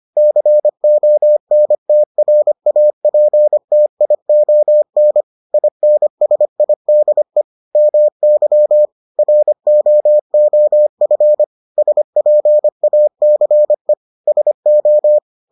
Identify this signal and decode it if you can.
Morse code, heard as a keyed tone, about 25 wpm, 600 Hz. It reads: CONTRAPTION INSIDE MY ROOF SPACE SO